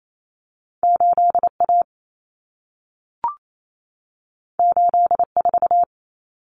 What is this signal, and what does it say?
Morse code, heard as a keyed tone, about 28 wpm, 700 Hz. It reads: 8A 84
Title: Common Cut Numbers - 2-Digit 28wpm